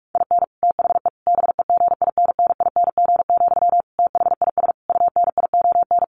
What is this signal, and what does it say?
Morse code, heard as a keyed tone, about 44 wpm, 700 Hz. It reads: IN THE BEGINNING, THIS UNION
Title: Sentences from Top 1400 Words 45wpm